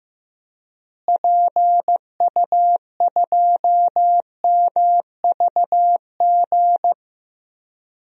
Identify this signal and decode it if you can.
Morse code, heard as a keyed tone, about 15 wpm, 700 Hz. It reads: PU2MVG
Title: DXCC-100 Call Signs - 6 Characters - Repeated 2x - 15wpm